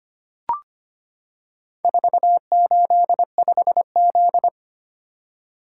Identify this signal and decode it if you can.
Morse code, heard as a keyed tone, about 25 wpm, 700 Hz. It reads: 485Z
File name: practice / 25wpm / Four Letter-Number Combinations - 25wpm